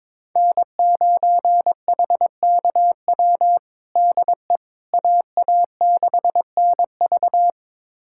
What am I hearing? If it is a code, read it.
Morse code, heard as a keyed tone, about 22 wpm, 700 Hz. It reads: N9HKW DE AA6NV
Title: USA Callsign DE Callsign - 5 Characters - 22wpm